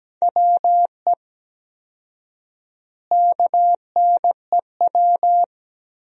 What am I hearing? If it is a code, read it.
Morse code, heard as a keyed tone, about 17 wpm, 700 Hz. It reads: WE KNEW
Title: 4x Word-Spacing - Sentences from Top 400 Words - 17wpm